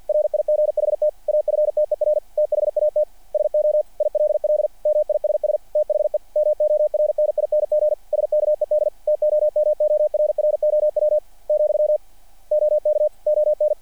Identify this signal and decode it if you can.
Morse code, heard as a keyed tone, about 48 wpm, 600 Hz. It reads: RIGHT AFTER THAT SO ILL MISS THE MORNING SKED TOMORROW , OK ON
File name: archived / CRSnet